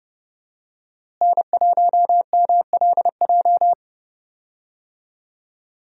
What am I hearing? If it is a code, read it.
Morse code, heard as a keyed tone, about 30 wpm, 700 Hz. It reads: N1MLJ
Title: USA Call Signs - 5 Characters - Repeated 3x - 30wpm